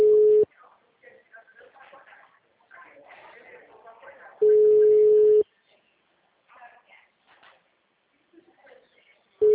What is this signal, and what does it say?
Ringback tone (Europe, 425 Hz), with speech.
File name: out- -2019- - - .16851